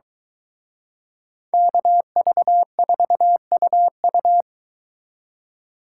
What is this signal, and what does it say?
Morse code, heard as a keyed tone, about 23 wpm, 700 Hz. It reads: KV4UU